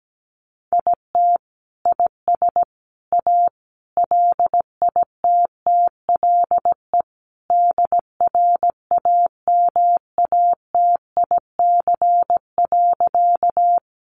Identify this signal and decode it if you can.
Morse code, heard as a keyed tone, about 17 wpm, 700 Hz. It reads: IT IS A LITTLE DRAMATIC.
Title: Sentences from Top 1300 Words 17wpm